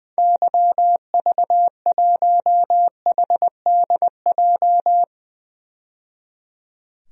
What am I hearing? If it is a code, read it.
Morse code, heard as a keyed tone, about 20 wpm, 700 Hz. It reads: YV1HDJ
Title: DXCC-100 Call Signs - 6 Characters - 20wpm